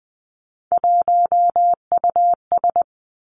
Morse code, heard as a keyed tone, about 20 wpm, 700 Hz. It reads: 1US
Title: Three Challenging Letter-Number Combinations 20wpm